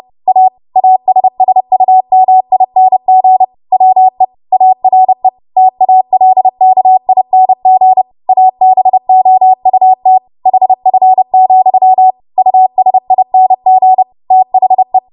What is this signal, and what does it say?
Morse code, heard as a keyed tone, about 30 wpm, 750 Hz. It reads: A ASSUMING WE ARE TALKING ABOUT HF, USING THE
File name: Morse / 30WPM